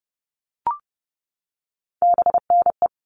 Morse code, heard as a keyed tone, about 30 wpm, 700 Hz. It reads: BNE
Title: All Cut Numbers - 3-Digit 30wpm